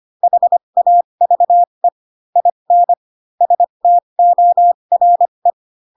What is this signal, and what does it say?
Morse code, heard as a keyed tone, about 25 wpm, spaced as for 22 wpm, 700 Hz. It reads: HAVE IN STORE
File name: Pinocchio / Pinocchio ch7 18 25 ik1hge